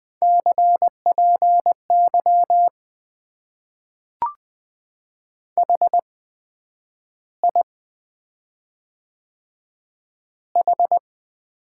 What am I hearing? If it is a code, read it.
Morse code, heard as a keyed tone, about 20 wpm, spaced as for 11 wpm, 700 Hz. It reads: CPY HI H
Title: Top 90 QSO Elements - 8x Character Spacing - 20wpm